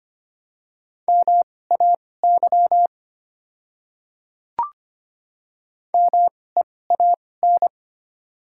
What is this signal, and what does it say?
Morse code, heard as a keyed tone, about 25 wpm, spaced as for 18 wpm, 700 Hz. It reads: MAY MEAN
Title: 2x Character Spacing - Top 300 Words - 25wpm